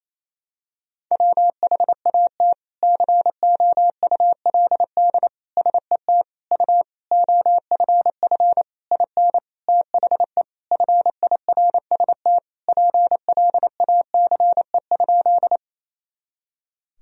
Morse code, heard as a keyed tone, about 28 wpm, 700 Hz. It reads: WHAT COULD SET U OFF IN THE FIRST PLACE?
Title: Sentences from Top 200 Words and Abbreviations 28wpm